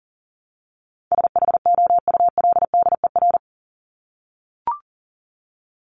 Morse code, heard as a keyed tone, about 40 wpm, 700 Hz. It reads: SHOULDER
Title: Top 1501-1750 Words 40wpm